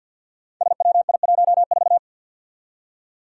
Morse code, heard as a keyed tone, about 48 wpm, 700 Hz. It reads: SWI14
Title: Five Challenging Letter-Number Combinations 50wpm